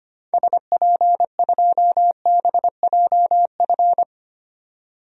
Morse code, heard as a keyed tone, about 25 wpm, 700 Hz. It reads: SP2BJF
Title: DXCC-100 Call Signs - 6 Characters - 25wpm